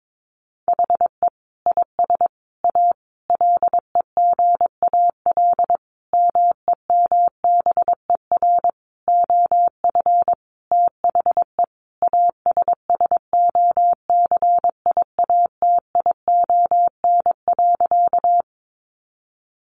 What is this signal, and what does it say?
Morse code, heard as a keyed tone, about 22 wpm, 700 Hz. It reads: HE IS A LEGAL MEMBER OF THE ASSOCIATION.
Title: Sentences from Top 1200 Words 22wpm